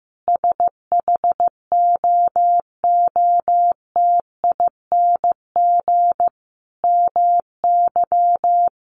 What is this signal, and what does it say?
Morse code, heard as a keyed tone, about 15 wpm, 700 Hz. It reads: SHOOTING MY
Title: A Sound Of Thunder 15wpm